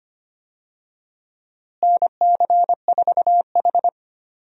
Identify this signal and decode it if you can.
Morse code, heard as a keyed tone, about 25 wpm, 700 Hz. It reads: NC4H
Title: USA Call Signs - 4 Characters - Repeated 3x - 25wpm